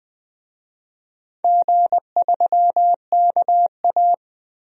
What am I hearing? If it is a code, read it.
Morse code, heard as a keyed tone, about 20 wpm, 700 Hz. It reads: G3KA
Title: DXCC-100 Call Signs - 4 Characters - 20wpm